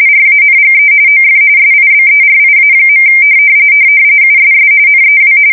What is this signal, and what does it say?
RTTY, about 45.45 baud. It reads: RANGE OF SIBERIA. USS NEW
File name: fsk170 / battleshipnj